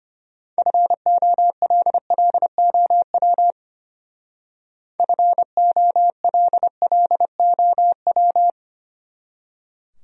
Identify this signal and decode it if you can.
Morse code, heard as a keyed tone, about 30 wpm, 700 Hz. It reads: FOLLOW FOLLOW
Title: Speed-Racing Top 200 Words 25wpm